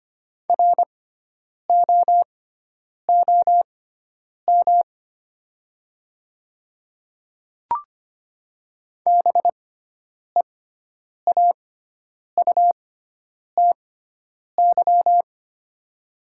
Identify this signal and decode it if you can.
Morse code, heard as a keyed tone, about 25 wpm, spaced as for 9 wpm, 700 Hz. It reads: ROOM BEAUTY
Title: 6x Character Spacing - Top 500 Words - 25wpm